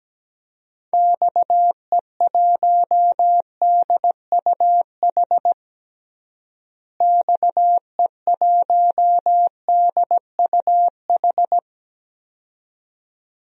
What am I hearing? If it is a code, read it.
Morse code, heard as a keyed tone, about 17 wpm, 700 Hz. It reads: XE1DUH XE1DUH
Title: DXCC-100 Call Signs - 6 Characters - Repeated 2x - 17wpm